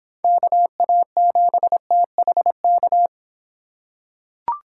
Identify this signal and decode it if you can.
Morse code, heard as a keyed tone, about 26 wpm, 700 Hz. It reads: KA7THK